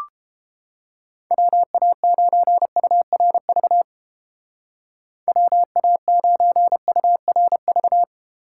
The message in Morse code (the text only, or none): WA9URV WA9URV